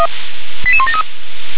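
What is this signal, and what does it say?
Five-tone selective call (ZVEI), heard as quick runs of five tones, tones of about 70 ms. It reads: 88173
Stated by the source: Five-tone ZVEI selective call received on an SDR.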